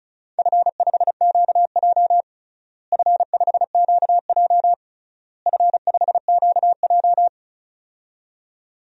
Morse code, heard as a keyed tone, about 36 wpm, 700 Hz. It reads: F5QJ F5QJ F5QJ
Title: DXCC-100 Call Signs - 4 Characters - Repeated 3x - 35wpm